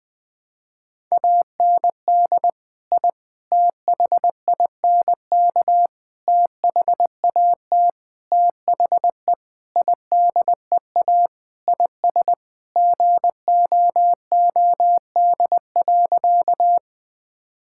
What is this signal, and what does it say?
Morse code, heard as a keyed tone, about 19 wpm, 700 Hz. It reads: AND I THINK THAT THE IDEA IS GOOD.
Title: Sentences with 5-Word Phrases - 20wpm